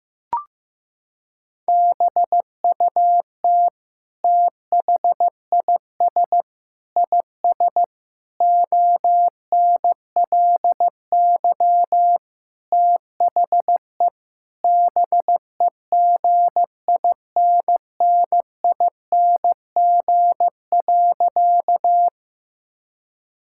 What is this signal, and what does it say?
Morse code, heard as a keyed tone, about 15 wpm, 700 Hz. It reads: BUT THIS IS ONLY THE BEGINNING.
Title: Sentences from Top 1000 Words - Review Part 5 - 15wpm